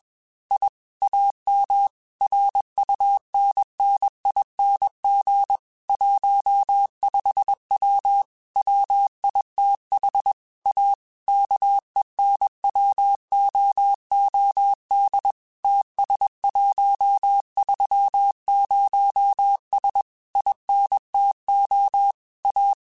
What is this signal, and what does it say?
Morse code, heard as a keyed tone, about 21 wpm, 800 Hz. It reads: I AM RUNNING 15W WITH A KENWOOD TS130S INTO A
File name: QSOs / QSOs-mp3s / WD0BGP 1@20